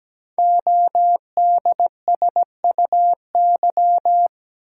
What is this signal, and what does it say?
Morse code, heard as a keyed tone, about 17 wpm, 700 Hz. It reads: ODSUY
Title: Code Groups - 5 Letters - 17wpm